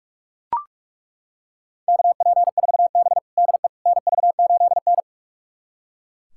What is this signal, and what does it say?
Morse code, heard as a keyed tone, about 41 wpm, 700 Hz. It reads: KP4B DE NV8N
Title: USA Callsign DE Callsign - 4 Characters - 45wpm